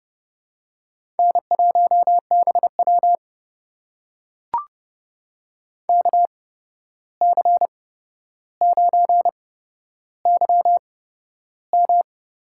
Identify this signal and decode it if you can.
Morse code, heard as a keyed tone, about 30 wpm, spaced as for 12 wpm, 700 Hz. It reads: N1BW KC9YM
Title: USA Call Signs - 8x Character Spacing - 30wpm